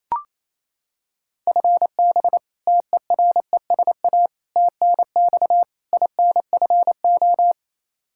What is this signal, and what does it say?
Morse code, heard as a keyed tone, about 28 wpm, 700 Hz. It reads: FB TERESA TNX INFO
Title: Practice QSO - Exchange 3 - More Info - 28wpm